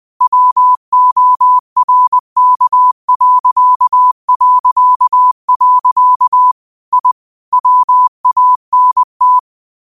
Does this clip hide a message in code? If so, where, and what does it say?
Yes: Morse code from 0.2 to 9.4 s, heard as a keyed tone, about 20 wpm, 1000 Hz. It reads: WORK... I WANT